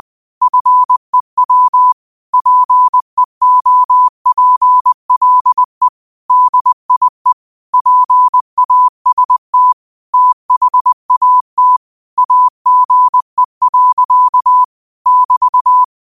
Morse code, heard as a keyed tone, about 20 wpm, 1000 Hz. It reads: FEW PEOPLE DIE PAST THAT AGE. <BT>
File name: qotdfiles / 20wpm